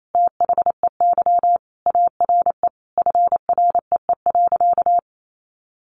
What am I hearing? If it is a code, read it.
Morse code, heard as a keyed tone, about 28 wpm, 700 Hz. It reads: THEY ARE FREE.
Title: Sentences from Top 1400 Words 28wpm